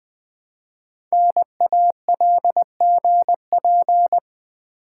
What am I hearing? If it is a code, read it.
Morse code, heard as a keyed tone, about 20 wpm, 700 Hz. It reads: NALGP
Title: Code Groups - 5 Letters - 20wpm